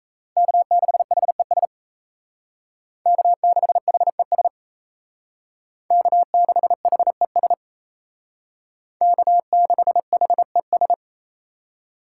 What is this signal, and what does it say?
Morse code, heard as a keyed tone, about 36 wpm, 700 Hz. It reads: K6HES K6HES K6HES K6HES